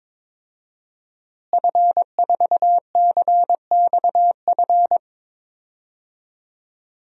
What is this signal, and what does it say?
Morse code, heard as a keyed tone, about 22 wpm, 700 Hz. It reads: F4CXF
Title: DXCC-100 Call Signs - 5 Characters - Repeated 3x - 22wpm